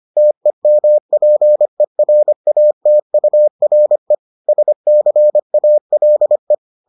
Morse code, heard as a keyed tone, about 25 wpm, 600 Hz. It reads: TEMPERATURE SCALE